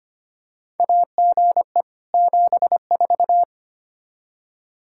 Morse code, heard as a keyed tone, about 25 wpm, 700 Hz. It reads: AGE 74
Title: Practice QSO - Exchange 3 - More Info - 25wpm